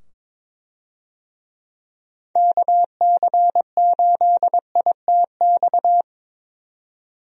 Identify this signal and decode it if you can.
Morse code, heard as a keyed tone, about 22 wpm, 700 Hz. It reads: KC8ITX
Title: USA Call Signs - 6 Characters - 22wpm